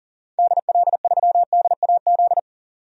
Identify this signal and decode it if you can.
Morse code, heard as a keyed tone, about 40 wpm, 700 Hz. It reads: DL3DAZ
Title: DXCC-100 Call Signs - 6 Characters - 40wpm